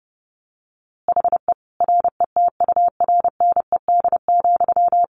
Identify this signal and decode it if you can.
Morse code, heard as a keyed tone, about 30 wpm, 700 Hz. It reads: HE RETURNED,
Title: A Sound Of Thunder 30wpm